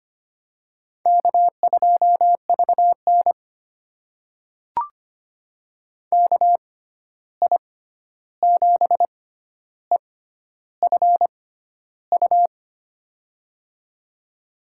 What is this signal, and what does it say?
Morse code, heard as a keyed tone, about 25 wpm, spaced as for 11 wpm, 700 Hz. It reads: K2VN KI7EFU
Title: USA Call Signs - 6x Character Spacing - 25wpm